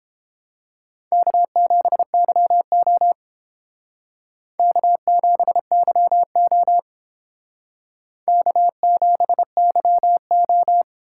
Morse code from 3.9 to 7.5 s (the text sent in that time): K7YO